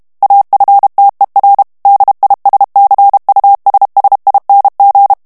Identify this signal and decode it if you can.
Morse code, heard as a keyed tone, about 32 wpm, 800 Hz. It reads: AFTER DISCUSSING
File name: morse mp3 / ANGLAIS texte clair 3 vitesse 30 mots minute